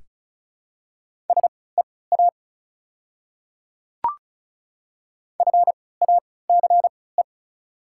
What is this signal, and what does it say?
Morse code, heard as a keyed tone, about 34 wpm, spaced as for 20 wpm, 700 Hz. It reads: SEA FACE
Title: 3x Character Spacing - Top 300 Words - 35wpm